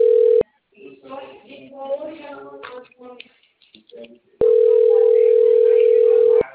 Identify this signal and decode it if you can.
Ringback tone (North America), 440 + 480 Hz.